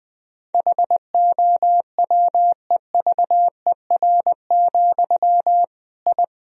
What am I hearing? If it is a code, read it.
Morse code, heard as a keyed tone, about 20 wpm, 700 Hz. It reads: HOWEVER, I
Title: Sentences from Top 1000 Words - Review Part 3 - 20wpm